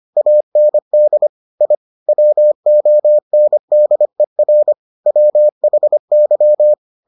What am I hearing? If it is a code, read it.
Morse code, heard as a keyed tone, about 25 wpm, 600 Hz. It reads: AND I WONDER WHY